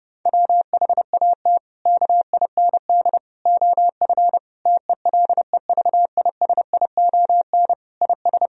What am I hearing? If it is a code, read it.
Morse code, heard as a keyed tone, about 30 wpm, 700 Hz. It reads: WHAT KIND OF TELEVISION IS
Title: Sentences from Top 1000 Words - Review Part 4 - 30wpm